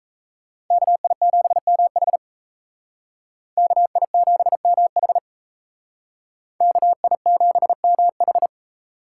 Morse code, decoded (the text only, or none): KI7MH KI7MH KI7MH